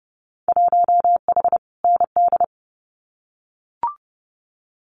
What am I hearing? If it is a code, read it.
Morse code, heard as a keyed tone, about 30 wpm, 700 Hz. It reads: 1H ND
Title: Winter Field Day Exchanges 30wpm